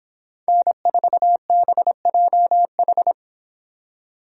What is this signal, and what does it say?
Morse code, heard as a keyed tone, about 26 wpm, 700 Hz. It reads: N4BJH